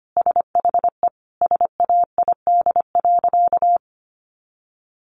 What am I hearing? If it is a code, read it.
Morse code, heard as a keyed tone, about 25 wpm, 700 Hz. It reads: SHE SAID.
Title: Sentences from Top 400 Words 25wpm